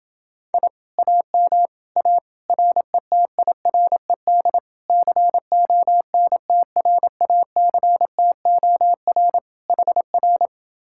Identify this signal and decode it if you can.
Morse code, heard as a keyed tone, about 27 wpm, 700 Hz. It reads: I AM A RETIRED CONTRACTOR HR